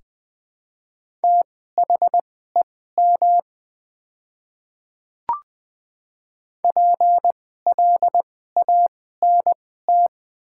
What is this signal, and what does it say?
Morse code, heard as a keyed tone, about 20 wpm, spaced as for 15 wpm, 700 Hz. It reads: THEM PLANT
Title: 2x Character Spacing - Top 300 Words - 20wpm